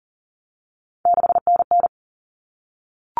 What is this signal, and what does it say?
Morse code, heard as a keyed tone, about 40 wpm, 700 Hz. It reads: 6NN
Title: Common Cut Numbers - 3-Digit 40wpm